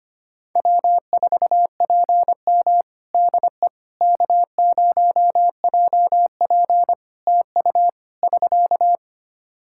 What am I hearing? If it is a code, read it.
Morse code, heard as a keyed tone, about 25 wpm, 700 Hz. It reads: W4PM DE K0JP TU <SK>
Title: Practice QSO - Exchange 4 - Ending - 25wpm